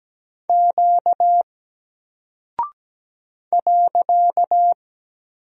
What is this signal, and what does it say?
Morse code, heard as a keyed tone, about 17 wpm, 700 Hz. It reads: Q .
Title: Single Character (Letter:Number:Punctuation:Prosign) 17wpm